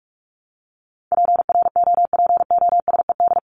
Morse code, heard as a keyed tone, about 44 wpm, 700 Hz. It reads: PROPOSED
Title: ED Suffix - Top 100 Words with Base Words 45wpm